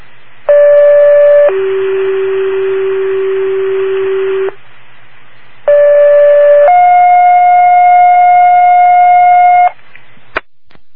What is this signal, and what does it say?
Two-tone page, heard as a one-second tone, then a three-second tone, about 580 then 380 Hz, about 580 then 720 Hz, with noise. It is heard from 0.5 to 9.7 s.